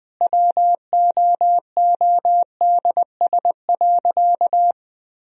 Morse code, heard as a keyed tone, about 20 wpm, 700 Hz. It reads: WOODS.
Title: Sentences from Top 1000 Words - Review Part 2 - 20wpm